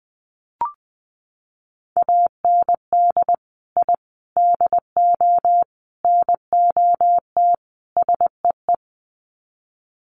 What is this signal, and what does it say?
Morse code, heard as a keyed tone, about 20 wpm, 700 Hz. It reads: AND I DO NOT SEE
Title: Sets of 5 Words Encore2 20wpm